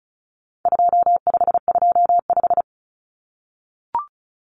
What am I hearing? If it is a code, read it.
Morse code, heard as a keyed tone, about 34 wpm, 700 Hz. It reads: 2525
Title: Common Cut Numbers - 4-Digit 35wpm